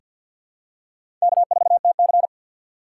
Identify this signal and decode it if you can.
Morse code, heard as a keyed tone, about 48 wpm, 700 Hz. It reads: K4TC
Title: USA Call Signs - 6x Character Spacing - 50wpm